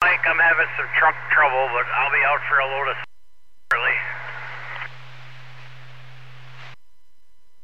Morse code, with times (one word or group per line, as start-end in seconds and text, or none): none